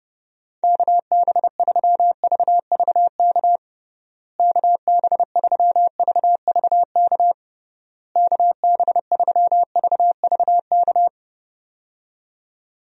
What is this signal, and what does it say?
Morse code, heard as a keyed tone, about 30 wpm, 700 Hz. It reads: KB3VVK KB3VVK KB3VVK
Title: USA Call Signs - 6 Characters - Repeated 3x - 30wpm